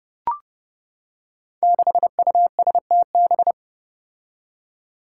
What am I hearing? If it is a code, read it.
Morse code, heard as a keyed tone, about 31 wpm, 700 Hz. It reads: 6USTB
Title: Five Challenging Letter-Number Combinations 30wpm